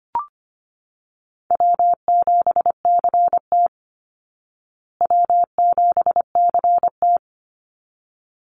Morse code, heard as a keyed tone, about 25 wpm, 700 Hz. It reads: W7CT W7CT
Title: Easy Call Signs Repeated 2x - 25wpm